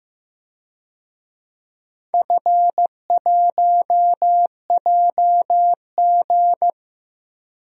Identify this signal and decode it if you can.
Morse code, heard as a keyed tone, about 15 wpm, 700 Hz. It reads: F1JG
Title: DXCC-100 Call Signs - 4 Characters - Repeated 3x - 15wpm